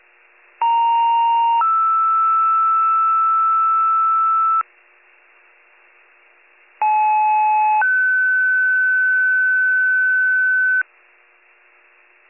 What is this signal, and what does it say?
Two-tone page, heard as a one-second tone, then a three-second tone, about 930 then 1320 Hz, about 860 then 1500 Hz. It is heard from 0.6 to 10.8 s.